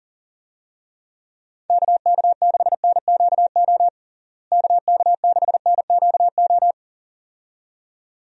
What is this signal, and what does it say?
Morse code, heard as a keyed tone, about 37 wpm, 700 Hz. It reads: KK6NQO KK6NQO
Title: USA Call Signs - 6 Characters - Repeated 2x - 40wpm